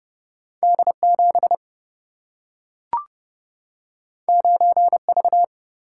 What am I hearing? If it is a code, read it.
Morse code, heard as a keyed tone, about 30 wpm, 700 Hz. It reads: D 7 9 V
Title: Single-Letter Single-Number 30wpm